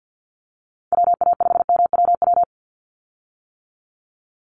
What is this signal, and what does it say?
Morse code, heard as a keyed tone, about 48 wpm, 700 Hz. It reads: WA5MWW
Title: USA Call Signs 50wpm